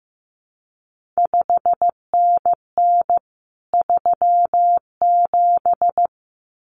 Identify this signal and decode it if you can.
Morse code, heard as a keyed tone, about 15 wpm, 700 Hz. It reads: 5NN 37
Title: CQ WW DX Contest Exchange Encore - 15wpm